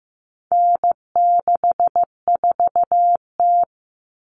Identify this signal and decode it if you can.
Morse code, heard as a keyed tone, about 15 wpm, 700 Hz. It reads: N64T
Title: Common Cut Numbers - 4-Digit 15wpm